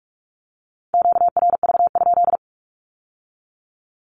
Morse code, heard as a keyed tone, about 46 wpm, 700 Hz. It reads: QRV?
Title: Q-Codes with Spoken Meaning - 45wpm